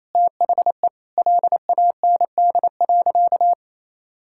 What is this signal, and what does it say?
Morse code, heard as a keyed tone, about 28 wpm, 700 Hz. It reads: THE LAND.
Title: Sentences from Top 1000 Words - Review Part 5 - 28wpm